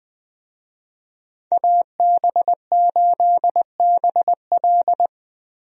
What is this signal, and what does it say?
Morse code, heard as a keyed tone, about 20 wpm, 700 Hz. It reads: AB8BL
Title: USA Call Signs - 5 Characters - Repeated 3x - 20wpm